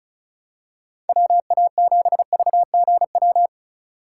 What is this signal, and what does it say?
Morse code, heard as a keyed tone, about 36 wpm, 700 Hz. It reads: WA7VGW
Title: USA Call Signs - 6 Characters - 2x Character Spacing - 35wpm